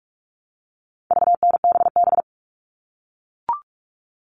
Morse code, heard as a keyed tone, about 44 wpm, 700 Hz. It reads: VNBB
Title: All Cut Numbers - 4-Digit 45wpm